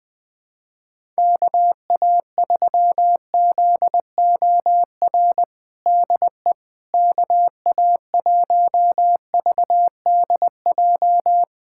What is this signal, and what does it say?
Morse code, heard as a keyed tone, about 20 wpm, 700 Hz. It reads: KA3ZOR DE KA1VDJ
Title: USA Callsign DE Callsign - 6 Characters - 20wpm